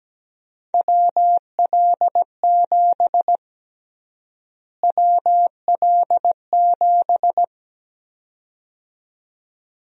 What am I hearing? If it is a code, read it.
Morse code, heard as a keyed tone, about 17 wpm, 700 Hz. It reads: WL7 WL7
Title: USA Call Sign Prefixes - Repeated 2x - 17wpm